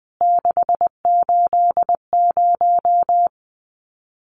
Morse code, heard as a keyed tone, about 19 wpm, 700 Hz. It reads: 680
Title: CWT Exchange 20wpm